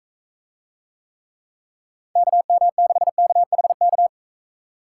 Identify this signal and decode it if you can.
Morse code, heard as a keyed tone, about 42 wpm, 700 Hz. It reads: KM6KHK